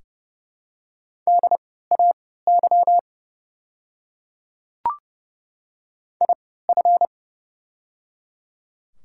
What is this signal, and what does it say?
Morse code, heard as a keyed tone, about 30 wpm, spaced as for 17 wpm, 690 Hz. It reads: DAY IF
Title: 3x Character Spacing - Top 300 Words - 30wpm